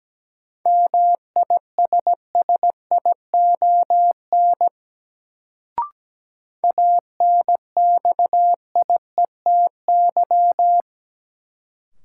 Morse code, heard as a keyed tone, about 17 wpm, 700 Hz. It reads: MISSION ANXIETY
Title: Sentences from Top 1600 Words - New Word Drill Alt - 17wpm